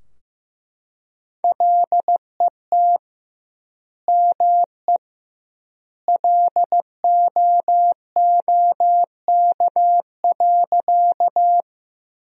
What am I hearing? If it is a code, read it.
Morse code, heard as a keyed tone, about 15 wpm, 700 Hz. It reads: LET ME LOOK.
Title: 2x Word-Spacing - Sentences from Top 300 Words - 15wpm